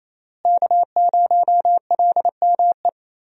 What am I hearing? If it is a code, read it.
Morse code, heard as a keyed tone, about 28 wpm, 700 Hz. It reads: K0LME
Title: USA Call Signs - 5 Characters - Repeated 2x - 28wpm